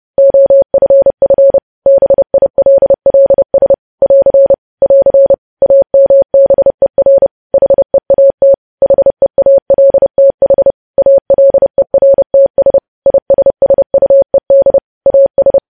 Morse code, heard as a keyed tone, about 30 wpm, 550 Hz. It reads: OFF BILLS <AR> <AR> AMBER HEAT HEALTH ALERTS ISSUED AS
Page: news headlines in morse code at 30 wpm